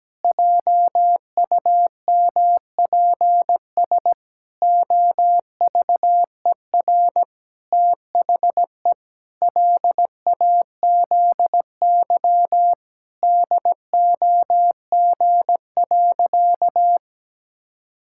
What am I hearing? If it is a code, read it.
Morse code, heard as a keyed tone, about 17 wpm, 700 Hz. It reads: JUMPS OVER THE LAZY DOG.
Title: Pangram Challenge v2 - 17wpm